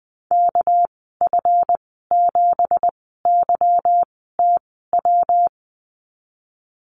Morse code, heard as a keyed tone, about 20 wpm, spaced as for 14 wpm, 700 Hz. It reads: KF7YTW
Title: 245 Lesson 39g - Letter z - Call Signs 20wpm